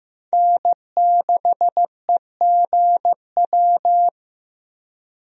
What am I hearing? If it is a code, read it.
Morse code, heard as a keyed tone, about 15 wpm, 700 Hz. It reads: N6EGW
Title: USA Call Signs - 5 Characters - Repeated 3x - 15wpm